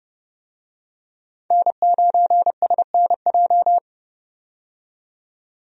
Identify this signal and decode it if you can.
Morse code, heard as a keyed tone, about 31 wpm, 700 Hz. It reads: N9SNJ
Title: USA Call Signs - 5 Characters - 30wpm